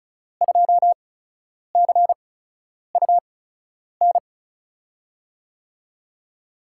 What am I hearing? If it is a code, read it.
Morse code, heard as a keyed tone, about 36 wpm, spaced as for 10 wpm, 700 Hz. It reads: 2CUN
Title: USA Call Signs - 8x Character Spacing - 35wpm